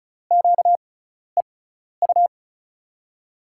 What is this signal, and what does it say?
Morse code, heard as a keyed tone, about 36 wpm, spaced as for 12 wpm, 700 Hz. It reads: QEU
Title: USA Call Signs - 6 Characters - 6x Character Spacing - 35wpm